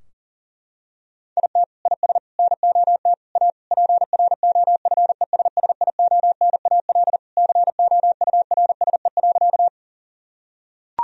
Morse code, heard as a keyed tone, about 40 wpm, 700 Hz. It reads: IT IS NOT A PROFESSIONAL COURSE.
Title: Sentences from Top 1000 Words - Review Part 4 - 40wpm